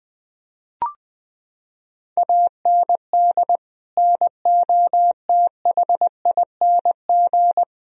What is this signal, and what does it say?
Morse code, heard as a keyed tone, about 20 wpm, 700 Hz. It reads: AND NOTHING
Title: Sentences from Top 1000 Words - Review Part 1 - 20wpm